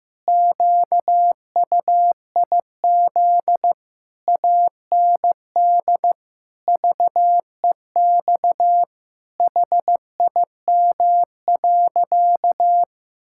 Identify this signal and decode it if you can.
Morse code, heard as a keyed tone, about 15 wpm, 700 Hz. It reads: QUIZ AND VEX HIM.
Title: Pangram Challenge v2 - 15wpm